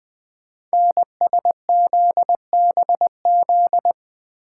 Morse code, heard as a keyed tone, about 20 wpm, 700 Hz. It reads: NSZBZ
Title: Code Groups - 5 Letters - 20wpm